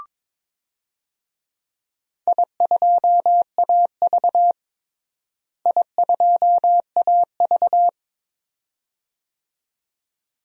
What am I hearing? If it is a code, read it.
Morse code, heard as a keyed tone, about 21 wpm, 700 Hz. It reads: I2AV I2AV
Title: DXCC-100 Call Signs - 4 Characters - Repeated 2x - 22wpm